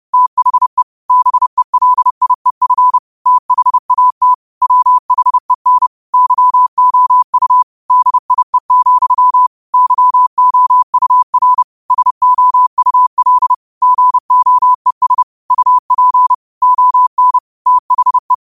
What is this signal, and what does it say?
Morse code, heard as a keyed tone, about 30 wpm, 1000 Hz. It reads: THE BELIEF THAT WHEN YOU DIE, YOUR SOUL GOES UP ON THE
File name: qotdfiles / 30wpm